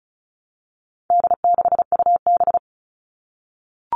Morse code, heard as a keyed tone, about 34 wpm, 700 Hz. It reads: D6UB
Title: All Cut Numbers - 4-Digit 35wpm